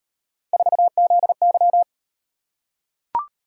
Morse code, heard as a keyed tone, about 37 wpm, 700 Hz. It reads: 4ZY